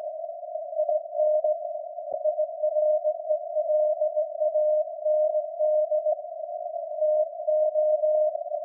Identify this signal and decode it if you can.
Morse code, heard as a keyed tone, about 17 wpm, 610 Hz. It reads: IN IRELAND TO
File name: iz7slz / VLF / SAQ jun 2019